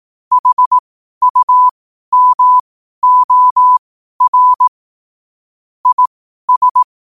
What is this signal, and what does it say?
Morse code, heard as a keyed tone, about 18 wpm, spaced as for 12 wpm, 1000 Hz. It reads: HUMOR IS
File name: qotdfiles / 12wpm